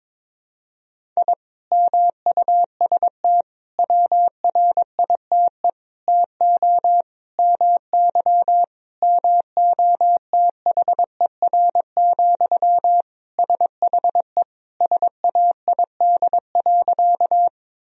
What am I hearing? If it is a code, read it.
Morse code, heard as a keyed tone, about 22 wpm, 700 Hz. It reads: I MUST WRITE TO MY MOTHER, SHE SAID.
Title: Sentences from Top 1000 Words - Review Part 1 - 22wpm